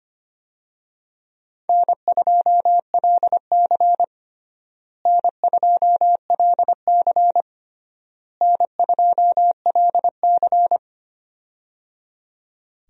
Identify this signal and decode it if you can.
Morse code, heard as a keyed tone, about 25 wpm, 700 Hz. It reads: N2LC N2LC N2LC
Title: USA Call Signs - 4 Characters - Repeated 3x - 25wpm